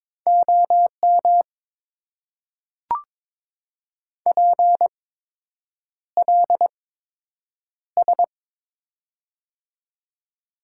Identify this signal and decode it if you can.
Morse code, heard as a keyed tone, about 22 wpm, spaced as for 8 wpm, 690 Hz. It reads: OM PLS
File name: Top 90 QSO Elements - 8x Character Spacing - 22wpm